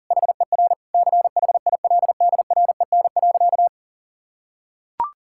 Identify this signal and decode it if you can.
Morse code, heard as a keyed tone, about 40 wpm, 700 Hz. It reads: HER CHILDREN.
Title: Sentences from Top 1000 Words - Review Part 6 - 40wpm